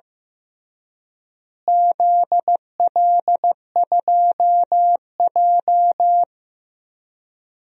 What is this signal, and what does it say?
Morse code, heard as a keyed tone, about 15 wpm, 700 Hz. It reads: ZL2J
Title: DXCC-100 Call Signs - 4 Characters - Repeated 3x - 15wpm